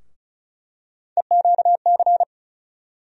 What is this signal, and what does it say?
Morse code, heard as a keyed tone, about 36 wpm, 700 Hz. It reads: EQC
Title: USA Call Sign Postfixes - 3 Letters 35wpm